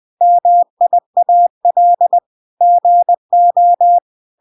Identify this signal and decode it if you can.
Morse code, heard as a keyed tone, about 20 wpm, 700 Hz. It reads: MIAL GO